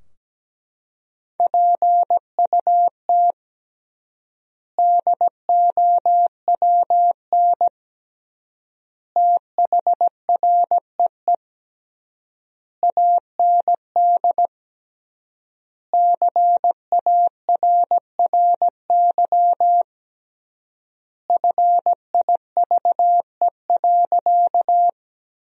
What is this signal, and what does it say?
Morse code, heard as a keyed tone, about 17 wpm, 700 Hz. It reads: PUT DOWN THREE AND CARRY FIVE.
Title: 3x Word-Spacing - Sentences from Top 400 Words - 17wpm